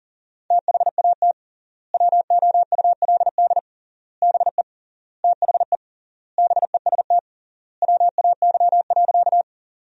Morse code, heard as a keyed tone, about 40 wpm, 700 Hz. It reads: THAT WOULD BE THE BEST WAY.
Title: 3x Word-Spacing - Sentences from Top 400 Words - 40wpm